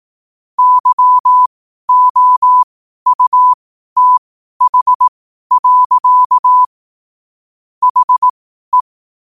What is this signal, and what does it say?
Morse code, heard as a keyed tone, about 18 wpm, spaced as for 12 wpm, 1000 Hz. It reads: YOUTH. HE
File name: qotdfiles / 12wpm